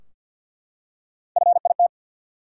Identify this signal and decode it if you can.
Morse code, heard as a keyed tone, about 51 wpm, 700 Hz. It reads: FIT
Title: Three Letter Words - 50wpm